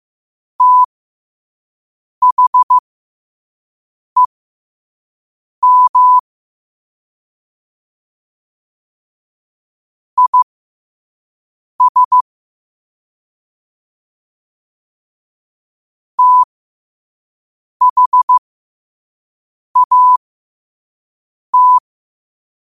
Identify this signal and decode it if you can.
Morse code, heard as a keyed tone, about 15 wpm, spaced as for 5 wpm, 1000 Hz. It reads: THEM IS THAT
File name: qotdfiles / 5wpm